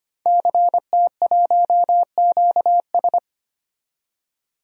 Morse code, heard as a keyed tone, about 25 wpm, 700 Hz. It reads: CT1QS
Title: DXCC-100 Call Signs - 5 Characters - 25wpm